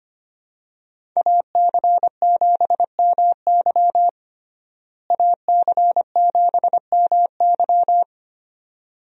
Morse code, heard as a keyed tone, about 25 wpm, 700 Hz. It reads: AC7MY AC7MY
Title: USA Call Signs - 5 Characters - Repeated 2x - 25wpm